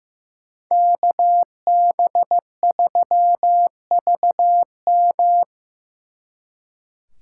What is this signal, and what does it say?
Morse code, heard as a keyed tone, about 18 wpm, 700 Hz. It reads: KB3VM
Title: Speed Racing - USA 2x2 Call Signs 15wpm